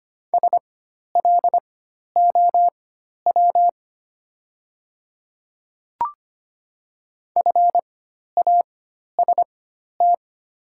Morse code, heard as a keyed tone, about 25 wpm, spaced as for 12 wpm, 700 Hz. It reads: SLOW FAST
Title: 4x Character Spacing - Top 500 Words - 25wpm